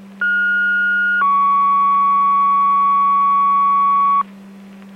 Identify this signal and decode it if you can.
Two-tone page, heard as a one-second tone, then a three-second tone, about 1430 then 1120 Hz. It is heard from 0.2 to 4.2 s.